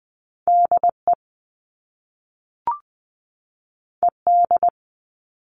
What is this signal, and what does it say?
Morse code, heard as a keyed tone, about 20 wpm, 700 Hz. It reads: DE ED
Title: All Cut Numbers - 2-Digit 20wpm